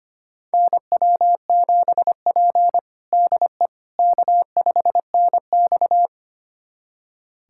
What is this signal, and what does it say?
Morse code, heard as a keyed tone, about 25 wpm, 700 Hz. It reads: NW7P DE K5NX
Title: USA Callsign DE Callsign - 4 Characters - 25wpm